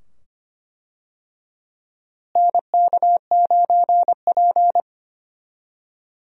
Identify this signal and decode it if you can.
Morse code, heard as a keyed tone, about 25 wpm, 700 Hz. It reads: NK9P
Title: USA Call Signs - 4 Characters - Repeated 2x - 25wpm